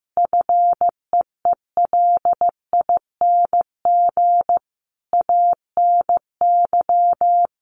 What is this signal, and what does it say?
Morse code, heard as a keyed tone, about 14 wpm, 700 Hz. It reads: FEELING ANY
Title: Sentences from Top 1200 Words 15wpm